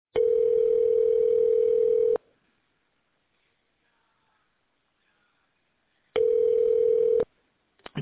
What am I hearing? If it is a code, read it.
Ringback tone (North America), 440 + 480 Hz.